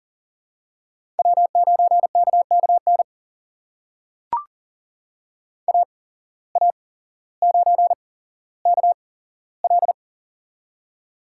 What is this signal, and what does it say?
Morse code, heard as a keyed tone, about 39 wpm, spaced as for 17 wpm, 700 Hz. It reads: W9KKN AA9KL
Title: USA Call Signs - 5 Characters - 8x Character Spacing - 40wpm